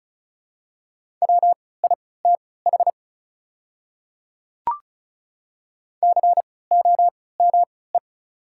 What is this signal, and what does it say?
Morse code, heard as a keyed tone, about 36 wpm, spaced as for 20 wpm, 700 Hz. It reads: WITH COME
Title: 3x Character Spacing - Top 100 Words - 35wpm